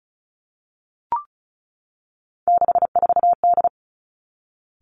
Morse code, heard as a keyed tone, about 36 wpm, 700 Hz. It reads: 64D
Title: Three Challenging Letter-Number Combinations 35wpm